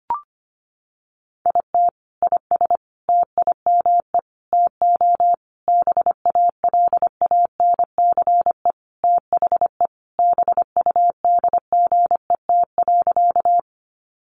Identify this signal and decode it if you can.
Morse code, heard as a keyed tone, about 25 wpm, 700 Hz. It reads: IT IS TIME TO BALANCE THE BUDGET.
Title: Sentences from Top 1200 Words 25wpm